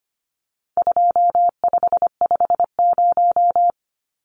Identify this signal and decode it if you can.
Morse code, heard as a keyed tone, about 25 wpm, 700 Hz. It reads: 2550
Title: Numbers - 4-Digit 25wpm